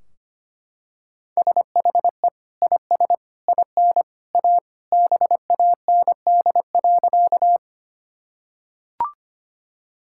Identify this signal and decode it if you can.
Morse code, heard as a keyed tone, about 25 wpm, 700 Hz. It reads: SHE IS IN A BAND.
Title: Sentences from Top 1600 Words 25wpm